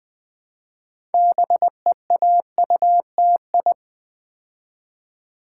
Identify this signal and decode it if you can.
Morse code, heard as a keyed tone, about 20 wpm, 700 Hz. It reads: BEAUTI
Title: Build A Word - 3 Letters At A Time - 9-Letter Words - 20wpm